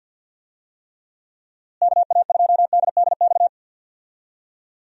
Morse code, heard as a keyed tone, about 46 wpm, 700 Hz. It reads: KA2DDX
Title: USA Call Signs - 6 Characters - Repeated 2x - 50wpm